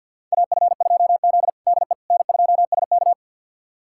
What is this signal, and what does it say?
Morse code, heard as a keyed tone, about 46 wpm, 690 Hz. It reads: AF2Z DE N2SK
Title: USA Callsign DE Callsign - 4 Characters - 50wpm